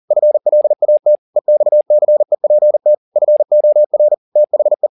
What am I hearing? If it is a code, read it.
Morse code, heard as a keyed tone, about 40 wpm, 600 Hz. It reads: FLAT EXCEPT FOR THE